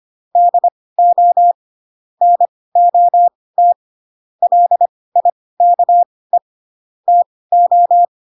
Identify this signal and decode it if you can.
Morse code, heard as a keyed tone, about 25 wpm, spaced as for 18 wpm, 700 Hz. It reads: DO NOT LIKE TO
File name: Pinocchio / Pinocchio ch4 12 25 ik1hge